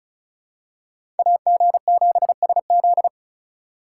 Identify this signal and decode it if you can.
Morse code, heard as a keyed tone, about 36 wpm, 700 Hz. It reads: AG7SZ
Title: USA Call Signs - 4x Character Spacing - 35wpm